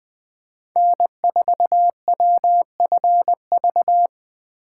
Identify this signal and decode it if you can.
Morse code, heard as a keyed tone, about 20 wpm, 700 Hz. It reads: N4WFV
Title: USA Call Signs - 5 Characters - 4x Character Spacing - 20wpm